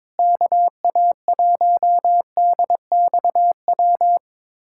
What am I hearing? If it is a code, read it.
Morse code, heard as a keyed tone, about 22 wpm, 700 Hz. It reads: KA1DXW
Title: USA Callsign DE Callsign - 6 Characters - 22wpm